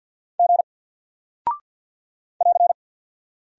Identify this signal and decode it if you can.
Morse code, heard as a keyed tone, about 48 wpm, 700 Hz. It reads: G <AR>
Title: Single Character (Letter:Number:Punctuation:Prosign) 50wpm